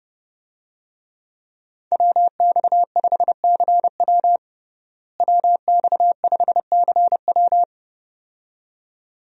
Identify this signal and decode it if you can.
Morse code, heard as a keyed tone, about 30 wpm, 700 Hz. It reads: WX5CW WX5CW
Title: USA Call Signs - 5 Characters - Repeated 2x - 30wpm